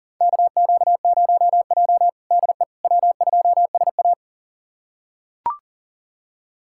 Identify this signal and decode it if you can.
Morse code, heard as a keyed tone, about 40 wpm, 700 Hz. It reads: KQ0J DE W2SA
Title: USA Callsign DE Callsign - 4 Characters - 40wpm